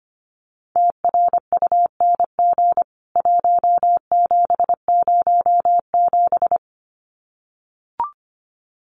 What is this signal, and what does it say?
Morse code, heard as a keyed tone, about 25 wpm, 700 Hz. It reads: TRUNG 1707
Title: CWT Exchange 25wpm